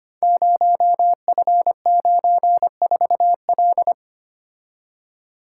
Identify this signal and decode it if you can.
Morse code, heard as a keyed tone, about 25 wpm, 700 Hz. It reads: 0F94L
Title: Code Groups - 5 Letter:Numbers - 25wpm